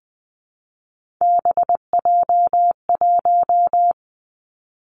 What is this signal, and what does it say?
Morse code, heard as a keyed tone, about 20 wpm, 700 Hz. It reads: BJ1
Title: Three Challenging Letter-Number Combinations 20wpm